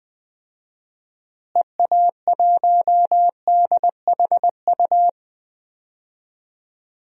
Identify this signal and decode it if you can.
Morse code, heard as a keyed tone, about 20 wpm, 700 Hz. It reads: EA1DHU
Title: DXCC-100 Call Signs - 6 Characters - Repeated 3x - 20wpm